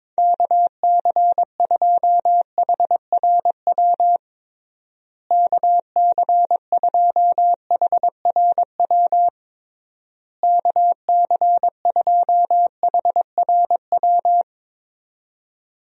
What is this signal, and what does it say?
Morse code, heard as a keyed tone, about 22 wpm, 700 Hz. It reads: KC2HRW KC2HRW KC2HRW
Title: USA Call Signs - 6 Characters - Repeated 3x - 22wpm